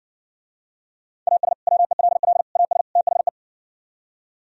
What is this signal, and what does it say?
Morse code, heard as a keyed tone, about 69 wpm, 700 Hz. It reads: AS WELL AS THE